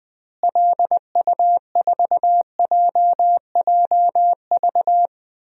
Morse code, heard as a keyed tone, about 21 wpm, 700 Hz. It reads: LU4JJV
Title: DXCC-100 Call Signs - 6 Characters - Repeated 3x - 20wpm